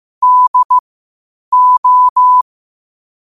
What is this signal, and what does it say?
Morse code, heard as a keyed tone, about 15 wpm, spaced as for 8 wpm, 1000 Hz. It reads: DO
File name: qotdfiles / 8wpm